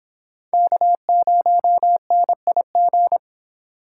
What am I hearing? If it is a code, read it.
Morse code, heard as a keyed tone, about 24 wpm, 700 Hz. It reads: K0NIG